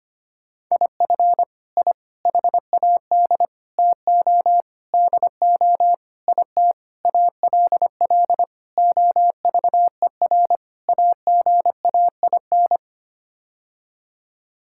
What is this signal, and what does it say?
Morse code, heard as a keyed tone, about 25 wpm, 700 Hz. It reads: IF I HAD TO DO IT ALL OVER AGAIN
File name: Sets of 9 Words - 25wpm